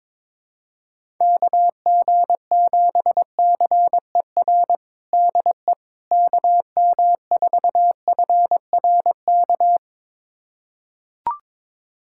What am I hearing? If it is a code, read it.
Morse code, heard as a keyed tone, about 22 wpm, 700 Hz. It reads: KG7CER DE KM4FRK
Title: USA Callsign DE Callsign 22wpm